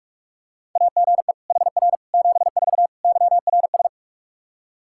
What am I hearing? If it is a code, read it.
Morse code, heard as a keyed tone, about 46 wpm, 700 Hz. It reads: AGE HR 74 YRS
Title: Practice QSO - Exchange 3 - More Info - 45wpm